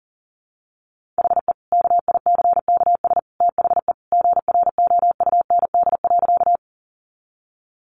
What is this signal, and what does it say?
Morse code, heard as a keyed tone, about 40 wpm, 700 Hz. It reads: HE KICKS THE GROUND.
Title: Additional Easy Phrases 40wpm